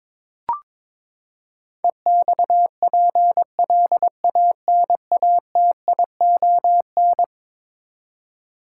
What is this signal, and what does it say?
Morse code, heard as a keyed tone, about 22 wpm, 700 Hz. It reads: EXPLANATION
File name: practice / 22wpm / Eleven Letter Words - 22wpm